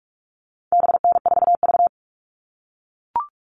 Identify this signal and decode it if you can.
Morse code, heard as a keyed tone, about 44 wpm, 700 Hz. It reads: BN4V
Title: All Cut Numbers - 4-Digit 45wpm